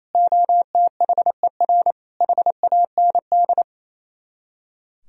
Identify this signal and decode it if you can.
Morse code, heard as a keyed tone, about 34 wpm, 700 Hz. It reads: OTHER HAND
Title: Speed-Racing - Sets of 3 Words 28wpm